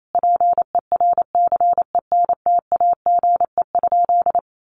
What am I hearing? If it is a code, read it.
Morse code, heard as a keyed tone, about 28 wpm, 700 Hz. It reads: PERCENTAGE?
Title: Sentences from Top 1300 Words 28wpm